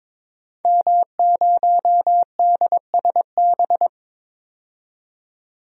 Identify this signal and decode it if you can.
Morse code, heard as a keyed tone, about 22 wpm, 700 Hz. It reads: M0DSB
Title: DXCC-100 Call Signs - 5 Characters - 22wpm